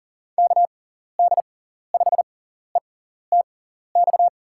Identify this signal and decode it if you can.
Morse code, heard as a keyed tone, about 40 wpm, spaced as for 14 wpm, 690 Hz. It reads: KD5ETX
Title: USA Call Signs - 6x Character Spacing - 40wpm